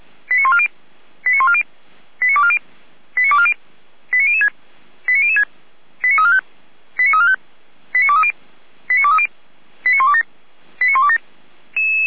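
Five-tone selective call (ZVEI), heard as quick runs of five tones, tones of about 80 ms. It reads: 89140 89140 89240 89240 89006 89006 89345 89345 89230 89230 89127 89127
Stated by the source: Received on SDR.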